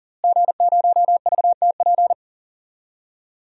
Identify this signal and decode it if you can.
Morse code, heard as a keyed tone, about 40 wpm, 700 Hz. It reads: G0VTP
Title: DXCC-100 Call Signs - 5 Characters - Repeated 3x - 40wpm